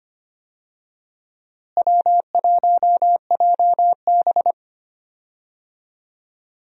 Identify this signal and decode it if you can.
Morse code, heard as a keyed tone, about 25 wpm, 700 Hz. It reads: W1JB
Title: USA Call Signs - 4 Characters - Repeated 2x - 25wpm